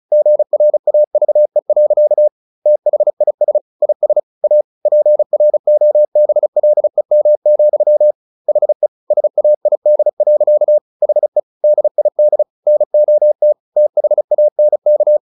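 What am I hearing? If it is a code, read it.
Morse code, heard as a keyed tone, about 34 wpm, 600 Hz. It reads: GRAVE. THIS IS A PROBLEM, HE SAID. HE DID NOT THANK
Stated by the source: Morse practice at 35 wpm from CW med Gnister, sending Ken Follett's novel Never in English.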